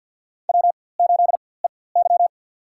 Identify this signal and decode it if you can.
Morse code, heard as a keyed tone, about 52 wpm, spaced as for 24 wpm, 700 Hz. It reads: W8EY
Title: USA Call Signs - 4x Character Spacing - 50wpm